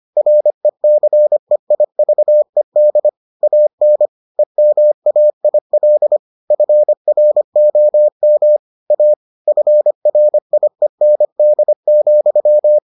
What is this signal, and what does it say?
Morse code, heard as a keyed tone, about 25 wpm, 600 Hz. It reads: RECEIVED AN EMAIL FROM A FRIEND,